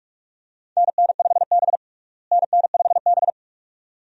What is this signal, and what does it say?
Morse code, heard as a keyed tone, about 43 wpm, 700 Hz. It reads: NN5B NN5B
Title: USA Call Signs - 4 Characters - Repeated 2x - 45wpm